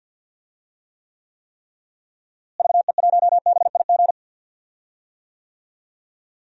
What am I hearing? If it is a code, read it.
Morse code, heard as a keyed tone, about 52 wpm, 700 Hz. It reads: VE1BIG
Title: DXCC-100 Call Signs - 6 Characters - Repeated 3x - 50wpm